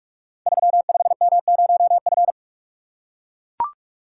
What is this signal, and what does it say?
Morse code, heard as a keyed tone, about 44 wpm, 700 Hz. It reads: 35M0F
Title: Code Groups - 5 Letter:Numbers - 45wpm